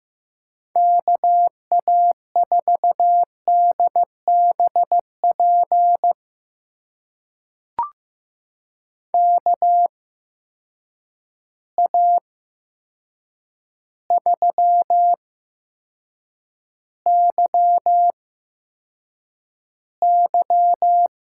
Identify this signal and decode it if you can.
Morse code, heard as a keyed tone, about 15 wpm, spaced as for 7 wpm, 700 Hz. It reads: KA4DBP KA3YY
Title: USA Call Signs - 6 Characters - 8x Character Spacing - 15wpm